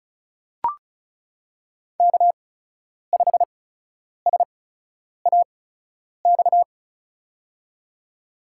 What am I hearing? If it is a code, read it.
Morse code, heard as a keyed tone, about 34 wpm, spaced as for 10 wpm, 700 Hz. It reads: K5SAX
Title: USA Call Signs - 5 Characters - 8x Character Spacing - 35wpm